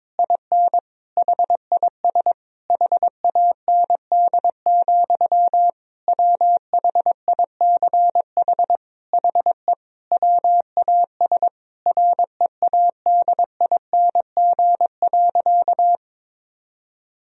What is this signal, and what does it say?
Morse code, heard as a keyed tone, about 22 wpm, 700 Hz. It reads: IN HIS HAND, WHICH HE WAS READING.
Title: Sentences from Top 1000 Words - Review Part 3 - 22wpm